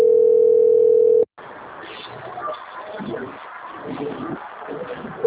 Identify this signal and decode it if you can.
Ringback tone (North America), 440 + 480 Hz.
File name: out- - - - - .26071